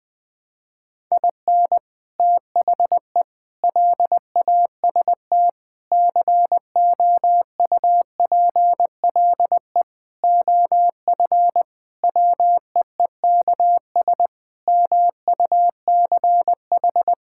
Morse code, heard as a keyed tone, about 20 wpm, 700 Hz. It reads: IN THE LAST COUPLE OF WEEKS MUCH
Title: Sentences with 5-Word Phrases - Spoken Beginning - 20wpm